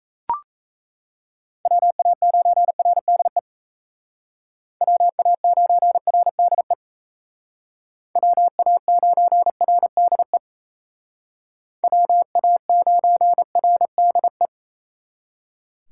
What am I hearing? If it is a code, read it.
Morse code, heard as a keyed tone, about 34 wpm, 700 Hz. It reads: WA9RDE WA9RDE WA9RDE WA9RDE
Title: Speed Racing - USA 2x3 Call Signs 28wpm